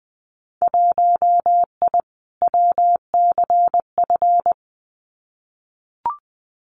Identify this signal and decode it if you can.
Morse code, heard as a keyed tone, about 20 wpm, 700 Hz. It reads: 1I WCF
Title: Winter Field Day Exchanges Encore 20wpm